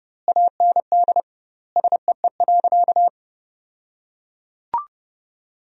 Morse code, heard as a keyed tone, about 30 wpm, 700 Hz. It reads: AND SEE.
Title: 2x Word-Spacing - Sentences from Top 100 Words - 30wpm